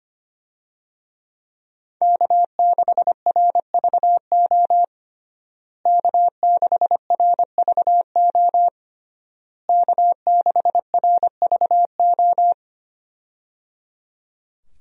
Morse code, heard as a keyed tone, about 25 wpm, 700 Hz. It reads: K6RVO K6RVO K6RVO
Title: USA Call Signs - 5 Characters - Repeated 3x - 25wpm